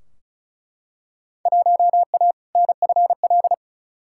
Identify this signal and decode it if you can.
Morse code, heard as a keyed tone, about 34 wpm, 700 Hz. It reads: 1A NFL
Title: Field Day Exchanges - 35wpm